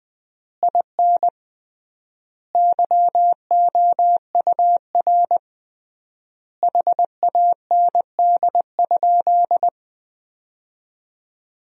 Morse code, heard as a keyed tone, about 20 wpm, 700 Hz. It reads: IN YOUR HAND?
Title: 3x Word-Spacing - Sentences from Top 200 Words - 20wpm